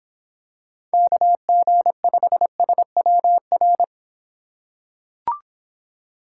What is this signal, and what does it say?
Morse code, heard as a keyed tone, about 26 wpm, 700 Hz. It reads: KG5SWR